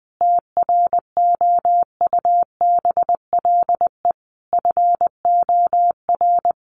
Morse code, heard as a keyed tone, about 20 wpm, 700 Hz. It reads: TROUBLE FOR
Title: Sentences from Top 1200 Words 20wpm